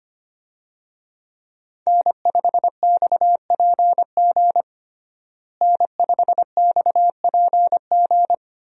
Morse code, heard as a keyed tone, about 25 wpm, 700 Hz. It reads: N5XPG N5XPG
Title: USA Call Signs - 5 Characters - Repeated 3x - 25wpm